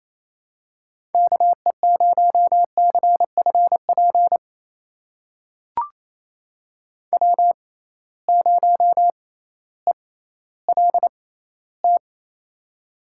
Morse code, heard as a keyed tone, about 28 wpm, spaced as for 15 wpm, 700 Hz. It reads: KE0CFP W0ELT
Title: USA Call Signs - 6x Character Spacing - 28wpm